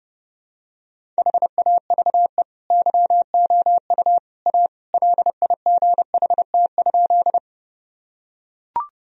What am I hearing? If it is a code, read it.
Morse code, heard as a keyed tone, about 30 wpm, 700 Hz. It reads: HAVE YOU A LIGHT?
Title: Sentences from Top 1000 Words - Review Part 1 - 30wpm